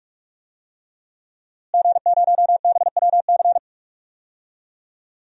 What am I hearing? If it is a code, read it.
Morse code, heard as a keyed tone, about 42 wpm, 700 Hz. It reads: G0BWC
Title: DXCC-100 Call Signs - 5 Characters - Repeated 3x - 45wpm